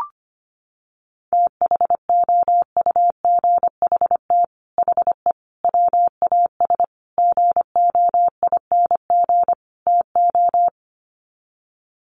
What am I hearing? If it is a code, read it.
Morse code, heard as a keyed tone, about 25 wpm, 700 Hz. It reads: THOUGHT HE WAS GOING TO
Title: Sets of 5 Words Encore2 25wpm